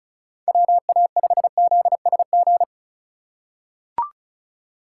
Morse code, heard as a keyed tone, about 36 wpm, 690 Hz. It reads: WA5ZSG
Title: USA Call Signs - 6 Characters - 2x Character Spacing - 35wpm